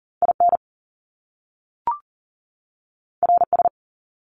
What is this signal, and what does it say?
Morse code, heard as a keyed tone, about 38 wpm, 700 Hz. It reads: IN RS
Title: Top Two Letter Combinations in Words 40wpm